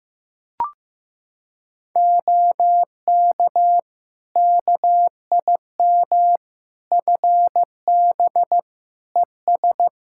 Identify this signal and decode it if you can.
Morse code, heard as a keyed tone, about 15 wpm, 700 Hz. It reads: OK KIM FB ES
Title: Practice QSO - Exchange 2 - Info - 15wpm